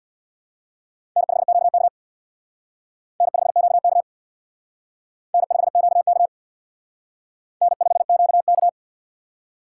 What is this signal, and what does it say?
Morse code, heard as a keyed tone, about 63 wpm, 700 Hz. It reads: N5QK N5QK N5QK N5QK